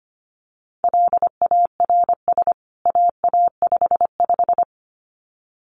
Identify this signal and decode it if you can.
Morse code, heard as a keyed tone, about 25 wpm, 700 Hz. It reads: LARS AA55
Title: CWT Exchange 25wpm